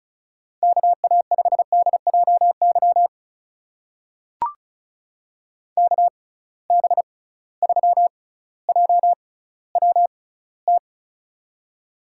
Morse code, heard as a keyed tone, about 36 wpm, spaced as for 18 wpm, 700 Hz. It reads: KA5DJY KB3JWT
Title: USA Call Signs - 6 Characters - 6x Character Spacing - 35wpm